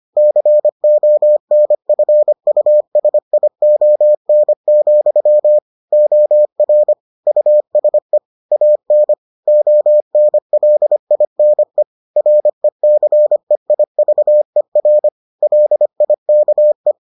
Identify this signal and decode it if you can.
Morse code, heard as a keyed tone, about 25 wpm, 600 Hz. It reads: CONFUSION, OR USE AN ONLINE RECEIVER LIKE